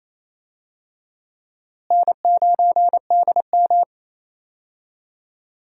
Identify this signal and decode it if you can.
Morse code, heard as a keyed tone, about 28 wpm, 700 Hz. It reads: N9DM
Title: USA Call Signs - 4 Characters - Repeated 2x - 28wpm